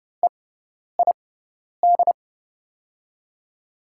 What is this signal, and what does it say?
Morse code, heard as a keyed tone, about 30 wpm, spaced as for 10 wpm, 700 Hz. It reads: EID
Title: USA Call Signs - 6x Character Spacing - 30wpm